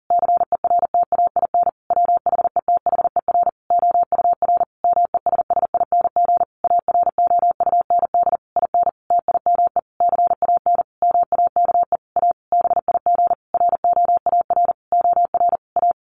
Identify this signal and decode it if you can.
Morse code, heard as a keyed tone, about 40 wpm, 700 Hz. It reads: CERTAIN WHETHER OUR MESSING AROUND IN TIME CAN MAKE A BIG ROAR OR A
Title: A Sound Of Thunder 40wpm